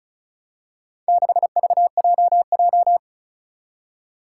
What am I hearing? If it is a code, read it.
Morse code, heard as a keyed tone, about 34 wpm, 700 Hz. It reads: 6VJJ
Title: Four Challenging Letter-Number Combinations 35wpm